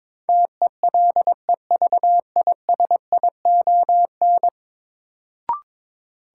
Morse code, heard as a keyed tone, about 21 wpm, 700 Hz. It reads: TELEVISION
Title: Sentences from Top 1000 Words - Review Part 4 - New Word Drill - 22wpm